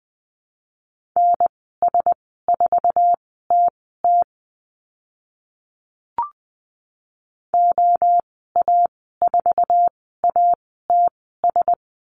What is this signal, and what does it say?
Morse code, heard as a keyed tone, about 20 wpm, spaced as for 14 wpm, 700 Hz. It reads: NS4TT OA4ATS
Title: 062 Lesson 11g - Letter h - Call Signs 20wpm